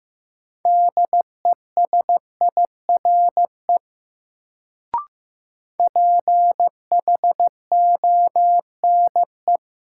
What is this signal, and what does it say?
Morse code, heard as a keyed tone, about 15 wpm, 700 Hz. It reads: DESIRE PHONE
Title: Sentences from Top 1100 Words - New Word Drill Alt - 15wpm